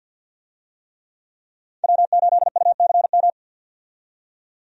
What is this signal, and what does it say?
Morse code, heard as a keyed tone, about 51 wpm, 700 Hz. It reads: W8UCM
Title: USA Call Signs - 5 Characters - 50wpm